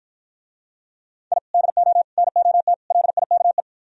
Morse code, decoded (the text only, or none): I DO NOT LIKE